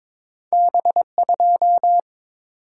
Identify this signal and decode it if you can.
Morse code, heard as a keyed tone, about 22 wpm, 700 Hz. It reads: B2
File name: practice / 22wpm / Two Letter-Number Combinations - 22wpm